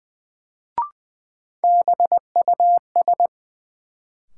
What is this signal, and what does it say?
Morse code, heard as a keyed tone, about 20 wpm, 700 Hz. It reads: BUS
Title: Sentences from Top 1000 Words - Review Part 6 - New Word Drill - 20wpm